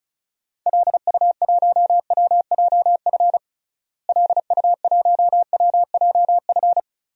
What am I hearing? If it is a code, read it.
Morse code, heard as a keyed tone, about 36 wpm, 700 Hz. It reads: LU1WJF LU1WJF
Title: DXCC-100 Call Signs - 6 Characters - Repeated 2x - 35wpm